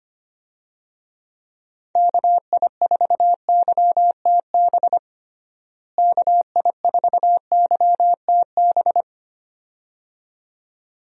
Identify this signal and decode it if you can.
Morse code, heard as a keyed tone, about 25 wpm, 700 Hz. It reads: KI4YTB KI4YTB
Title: USA Call Signs - 6 Characters - Repeated 2x - 25wpm